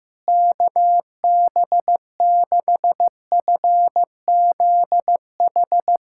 Morse code, heard as a keyed tone, about 16 wpm, 700 Hz. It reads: KB6FZH
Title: USA Callsign DE Callsign 15wpm